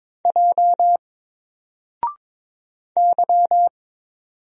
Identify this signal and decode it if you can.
Morse code, heard as a keyed tone, about 22 wpm, 700 Hz. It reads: J Y
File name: Single-Letter-Number-22wpm